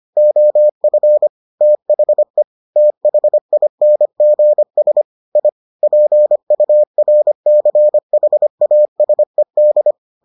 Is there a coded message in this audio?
Yes: Morse code from 0.2 to 9.9 s, heard as a keyed tone, about 25 wpm, 600 Hz. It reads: OF THE THINGS I PURCHASED